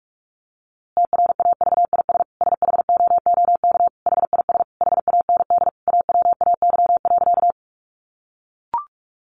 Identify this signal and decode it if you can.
Morse code, heard as a keyed tone, about 44 wpm, 700 Hz. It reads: TRAVIS SHOOK HIS HAND AWAY.
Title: A Sound Of Thunder 45wpm